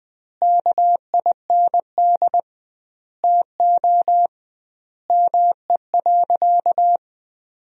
Morse code, heard as a keyed tone, about 20 wpm, 700 Hz. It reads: KIND TO ME.
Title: 2x Word-Spacing - Sentences from Top 200 Words - 20wpm